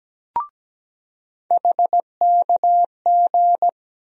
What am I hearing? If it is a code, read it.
Morse code, heard as a keyed tone, about 17 wpm, 700 Hz. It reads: HKG
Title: USA Call Sign Postfixes - 3 Letters 17wpm